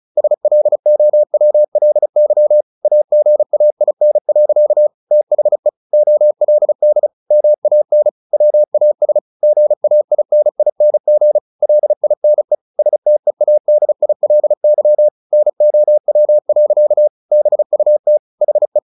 Morse code, heard as a keyed tone, about 34 wpm, 600 Hz. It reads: SLOWLY AGAIN. THE OLD MAN WAS GAINING LINE STEADILY NOW. BUT HE
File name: the old man and the sea 2 35wpm